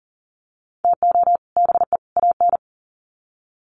Morse code, heard as a keyed tone, about 40 wpm, 700 Hz. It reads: TO BE AN
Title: Sets of 3 Words 40wpm